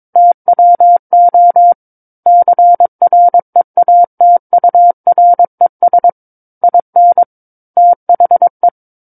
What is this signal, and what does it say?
Morse code, heard as a keyed tone, about 22 wpm, 700 Hz. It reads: TWO CREATURES IN THE
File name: 06-Chapter-06 22 WPM